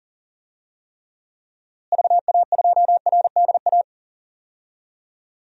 Morse code, heard as a keyed tone, about 40 wpm, 700 Hz. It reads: VA2RDA
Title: DXCC-100 Call Signs - 6 Characters - Repeated 3x - 40wpm